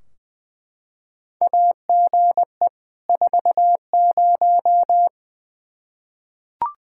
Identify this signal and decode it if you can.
Morse code, heard as a keyed tone, about 20 wpm, 700 Hz. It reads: AGE 40
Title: Practice QSO - Exchange 3 - More Info - 20wpm